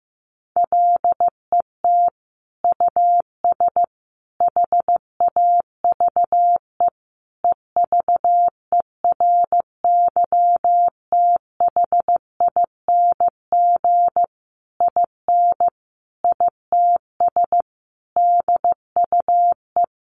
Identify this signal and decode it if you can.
Morse code, heard as a keyed tone, about 15 wpm, 700 Hz. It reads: LET US HAVE EVERYTHING IN ITS DUE
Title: Sentences from Top 900 Words 15wpm